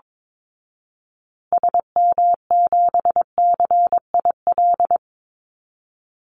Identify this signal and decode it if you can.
Morse code, heard as a keyed tone, about 22 wpm, 700 Hz. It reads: SM7CIL
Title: Call Signs Repeated 2x - 22wpm